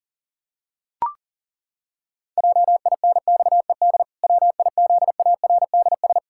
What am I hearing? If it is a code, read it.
Morse code, heard as a keyed tone, about 40 wpm, 700 Hz. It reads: JINXED WIZARDS
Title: Pangram Challenge v2 - 40wpm